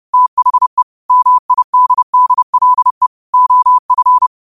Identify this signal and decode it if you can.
Morse code, heard as a keyed tone, about 31 wpm, 1000 Hz. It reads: THE MIDDLE OF
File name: qotdfiles / 30wpm